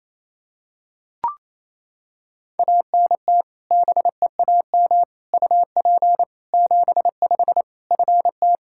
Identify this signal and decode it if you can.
Morse code, heard as a keyed tone, about 28 wpm, 700 Hz. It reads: ANT BEAM UP 75 FT
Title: Practice QSO - Exchange 2 - Info - 28wpm